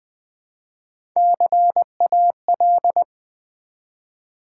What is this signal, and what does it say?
Morse code, heard as a keyed tone, about 20 wpm, 700 Hz. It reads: CAL
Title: Build A Word - 1 Letter At A Time - 4-Letter Words - 20wpm